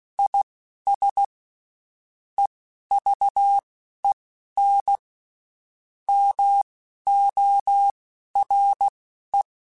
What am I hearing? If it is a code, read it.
Morse code, heard as a keyed tone, about 16 wpm, spaced as for 11 wpm, 800 Hz. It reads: IS EVEN MORE
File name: Spaced / Chuckles Spaced-mp3s / HumanErr sp @15